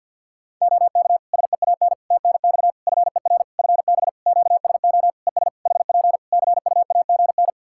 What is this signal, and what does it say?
Morse code, heard as a keyed tone, about 52 wpm, 700 Hz. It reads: OK SEAN TNX FER FB QSO ES HP CUAGN
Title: Practice QSO - Exchange 4 - Ending - 50wpm